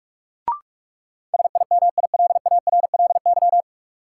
Morse code, heard as a keyed tone, about 47 wpm, 700 Hz. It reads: SIMILARLY
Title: Sentences from Top 1400 Words - New Word Drill - 45wpm